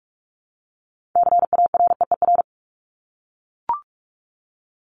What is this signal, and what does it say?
Morse code, heard as a keyed tone, about 44 wpm, 700 Hz. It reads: CAREER
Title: Top 751-1000 Words 45wpm